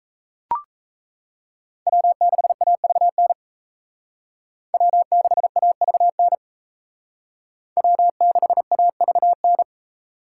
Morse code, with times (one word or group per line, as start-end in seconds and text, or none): 1.9-3.3 W6AVN
4.7-6.3 W6AVN
7.8-9.6 W6AVN